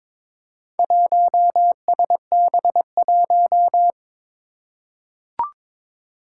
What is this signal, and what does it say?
Morse code, heard as a keyed tone, about 21 wpm, 700 Hz. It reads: 1SB1
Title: Four Challenging Letter-Number Combinations 22wpm